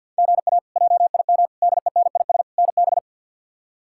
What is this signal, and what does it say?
Morse code, heard as a keyed tone, about 52 wpm, 690 Hz. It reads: GA JIM DENIS NB
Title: K1USN SST QSO Practice - 50wpm